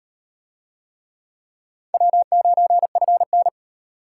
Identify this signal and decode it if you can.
Morse code, heard as a keyed tone, about 38 wpm, 700 Hz. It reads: W9FN